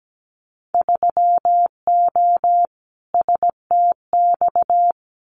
Morse code, heard as a keyed tone, about 17 wpm, 700 Hz. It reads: 3O STX
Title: Winter Field Day Exchanges 17wpm